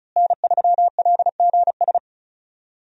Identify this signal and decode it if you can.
Morse code, heard as a keyed tone, about 36 wpm, 700 Hz. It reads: N3LGS
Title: USA Call Signs - 5 Characters - Repeated 3x - 35wpm